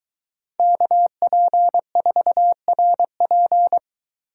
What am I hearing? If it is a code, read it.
Morse code, heard as a keyed tone, about 22 wpm, 700 Hz. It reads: KP4RP